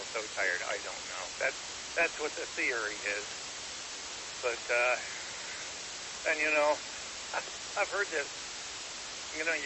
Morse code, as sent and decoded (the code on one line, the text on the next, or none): none